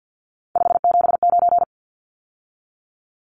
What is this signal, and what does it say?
Morse code, heard as a keyed tone, about 49 wpm, 700 Hz. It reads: 579
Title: Numbers - 3-Digit 50wpm